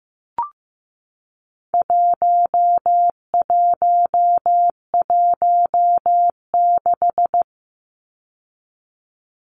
Morse code, heard as a keyed tone, about 15 wpm, 700 Hz. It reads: 1116
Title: Numbers - 4-Digit 15wpm